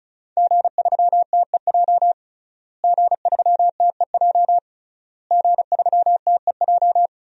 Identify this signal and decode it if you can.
Morse code, heard as a keyed tone, about 36 wpm, 700 Hz. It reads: G3TEJ G3TEJ G3TEJ
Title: DXCC-100 Call Signs - 5 Characters - Repeated 3x - 35wpm